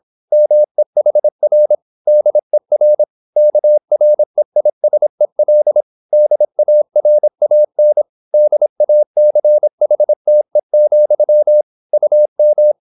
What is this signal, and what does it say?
Morse code, heard as a keyed tone, about 26 wpm, 600 Hz. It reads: MEHR DER KREISEL DARAN DACHTE, UM